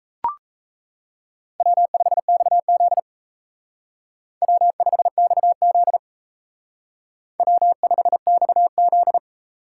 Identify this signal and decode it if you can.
Morse code, heard as a keyed tone, about 34 wpm, 700 Hz. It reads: W5XZ W5XZ W5XZ
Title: Speed Racing - USA 1x2 Call Signs 28wpm